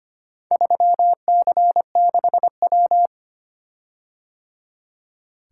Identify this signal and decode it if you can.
Morse code, heard as a keyed tone, about 25 wpm, 700 Hz. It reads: 3C6W
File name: practice / 25wpm / Four Letter-Number Combinations - 25wpm